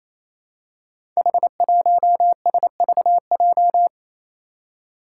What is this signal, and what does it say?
Morse code, heard as a keyed tone, about 27 wpm, 700 Hz. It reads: H1SVJ
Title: Five Challenging Letter-Number Combinations 28wpm